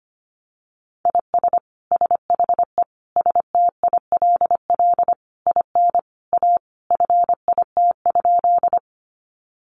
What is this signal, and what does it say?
Morse code, heard as a keyed tone, about 25 wpm, 700 Hz. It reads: IS SHE STILL IN A FIT?
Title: Sentences from Top 800 Words 25wpm